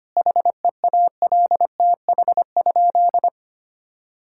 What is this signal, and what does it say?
Morse code, heard as a keyed tone, about 25 wpm, 700 Hz. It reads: HEALTH?
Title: Sentences from Top 1000 Words - Review Part 5 - 25wpm